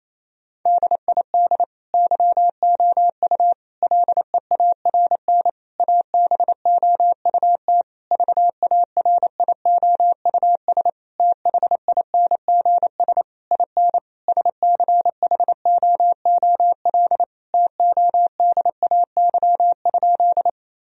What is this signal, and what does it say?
Morse code, heard as a keyed tone, about 28 wpm, 700 Hz. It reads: DID YOU LEARN ABOUT VARIOUS THINGS IN SCHOOL TODAY?
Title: Sentences from Top 1000 Words - Review Part 4 - 28wpm